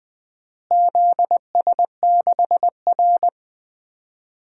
Morse code, heard as a keyed tone, about 20 wpm, 700 Hz. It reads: ZS6R
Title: DXCC-100 Call Signs - 4 Characters - Repeated 2x - 20wpm